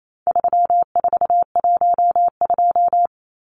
Morse code, heard as a keyed tone, about 28 wpm, 700 Hz. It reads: 3412
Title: Numbers - 4-Digit 28wpm